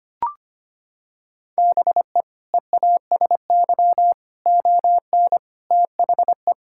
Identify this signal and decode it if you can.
Morse code, heard as a keyed tone, about 25 wpm, 700 Hz. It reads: BE EASY ON THE
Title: Sentences from Top 1600 Words 25wpm